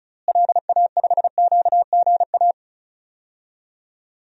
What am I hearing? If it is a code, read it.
Morse code, heard as a keyed tone, about 36 wpm, 700 Hz. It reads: LA5QGA
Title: DXCC-100 Call Signs - 6 Characters - Repeated 2x - 35wpm